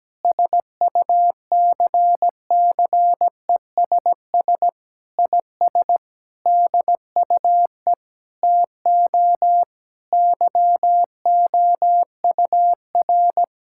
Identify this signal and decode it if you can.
Morse code, heard as a keyed tone, about 17 wpm, 700 Hz. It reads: SUCCESS IS DUE TO YOUR
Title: Sentences from Top 1000 Words - Review Part 6 - 17wpm